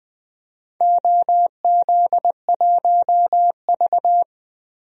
Morse code, heard as a keyed tone, about 20 wpm, 700 Hz. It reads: OZ1V
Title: DXCC-100 Call Signs - 4 Characters - 20wpm